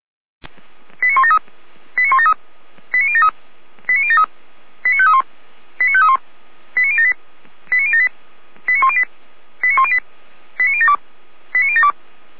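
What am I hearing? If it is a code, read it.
Five-tone selective call (ZVEI), heard as quick runs of five tones, tones of about 70 ms. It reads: 89173 89173 89073 89073 89531 89531 89078 89078 89108 89108 89072 89072
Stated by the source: ZVEI five-tone selective call 89337 received by SDR, from the Feuerwehr Orlamünde (fire brigade) audio server.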